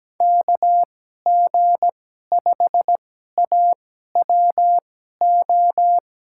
Morse code, heard as a keyed tone, about 17 wpm, spaced as for 12 wpm, 700 Hz. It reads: KG5AWO
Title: USA Call Signs - 6 Characters - 2x Character Spacing - 17wpm